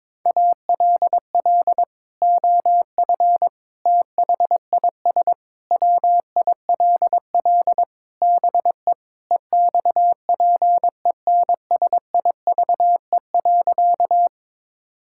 Morse code, heard as a keyed tone, about 22 wpm, 700 Hz. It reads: ALL OF THIS WILL BE EXPENSIVE.
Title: Sentences from Top 1000 Words - Review Part 6 - 22wpm